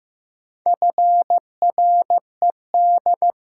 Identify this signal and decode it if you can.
Morse code, heard as a keyed tone, about 16 wpm, 700 Hz. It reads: FRED
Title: Pangram Challenge v2 - 15wpm